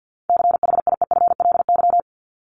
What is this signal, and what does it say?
Morse code, heard as a keyed tone, about 50 wpm, 700 Hz. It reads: CHIEFLY
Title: LY Suffix - Top 100 Words with Base Words 50wpm